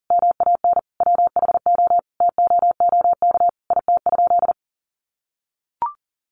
Morse code, heard as a keyed tone, about 40 wpm, 700 Hz. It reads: MAN WHO TOOK IT?
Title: Sentences from Top 300 Words 40wpm